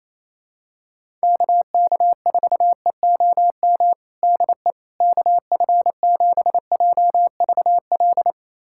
Morse code, heard as a keyed tone, about 28 wpm, 700 Hz. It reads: KK4EOM DE KF7JVL
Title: USA Callsign DE Callsign - 6 Characters - 28wpm